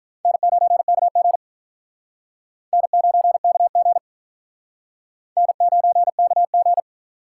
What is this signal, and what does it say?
Morse code, heard as a keyed tone, about 42 wpm, 700 Hz. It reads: N9KG N9KG N9KG
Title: Speed Racing - USA Call Signs 35wpm